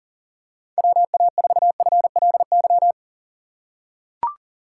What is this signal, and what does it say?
Morse code, heard as a keyed tone, about 40 wpm, 700 Hz. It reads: WA4FLY
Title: USA Call Signs - 6 Characters - 3x Character Spacing - 40wpm